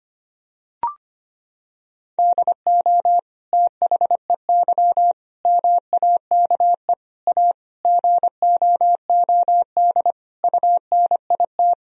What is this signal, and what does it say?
Morse code, heard as a keyed tone, about 25 wpm, 700 Hz. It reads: DO THEY MAKE A GOOD UNIT
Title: Sentences from Top 1000 Words - Review Part 2 - 25wpm